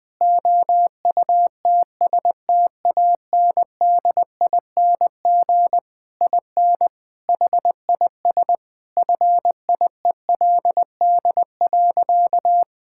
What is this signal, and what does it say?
Morse code, heard as a keyed tone, about 20 wpm, 700 Hz. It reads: OUTSTANDING IN HIS FIELD.
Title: All-Age Appropriate Jokes - 20wpm